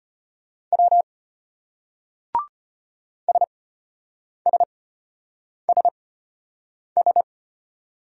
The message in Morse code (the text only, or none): W SSSS